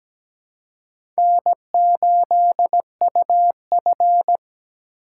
Morse code, heard as a keyed tone, about 17 wpm, 700 Hz. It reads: N8UF
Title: USA Call Signs - 4 Characters - 4x Character Spacing - 17wpm